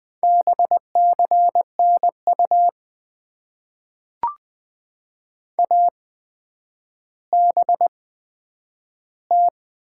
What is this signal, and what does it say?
Morse code, heard as a keyed tone, about 20 wpm, spaced as for 9 wpm, 700 Hz. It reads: BCN U ABT
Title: Top 90 QSO Elements - 8x Character Spacing - 20wpm